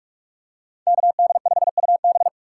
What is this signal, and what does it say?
Morse code, heard as a keyed tone, about 46 wpm, 700 Hz. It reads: KD5UB